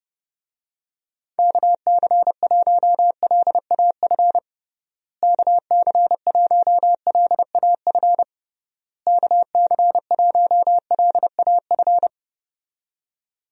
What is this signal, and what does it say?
Morse code, heard as a keyed tone, about 30 wpm, 700 Hz. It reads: KC1LAF KC1LAF KC1LAF
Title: USA Call Signs - 6 Characters - Repeated 3x - 30wpm